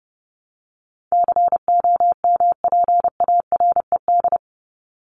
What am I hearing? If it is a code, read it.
Morse code, heard as a keyed tone, about 29 wpm, 700 Hz. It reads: COMPARED
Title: ED Suffix - Top 100 Words with Base Words 30wpm